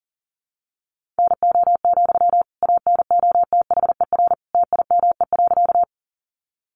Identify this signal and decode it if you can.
Morse code, heard as a keyed tone, about 40 wpm, 700 Hz. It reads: NO, ANOTHER TIME.
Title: Sentences from Top 600 Words 40wpm